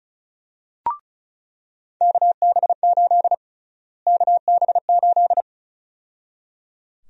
Morse code, heard as a keyed tone, about 36 wpm, 700 Hz. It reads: KB8 KB8
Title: USA Call Sign Prefixes - Repeated 2x - 35wpm